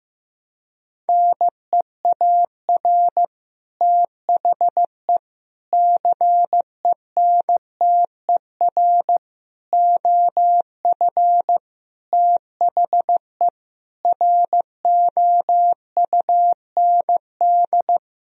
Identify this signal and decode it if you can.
Morse code, heard as a keyed tone, about 15 wpm, 700 Hz. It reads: NEAR THE CENTER OF THE ROUND
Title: Sentences from Top 1000 Words - Review Part 2 - 15wpm